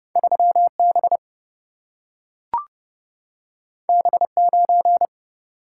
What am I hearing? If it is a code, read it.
Morse code, heard as a keyed tone, about 30 wpm, 700 Hz. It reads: 3 B B 9
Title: Single-Letter Single-Number 30wpm